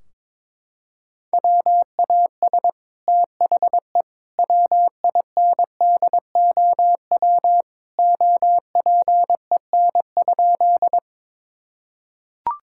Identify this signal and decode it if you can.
Morse code, heard as a keyed tone, about 21 wpm, 700 Hz. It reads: WAS THE WINDOW OPEN?
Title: Sentences from Top 1600 Words 22wpm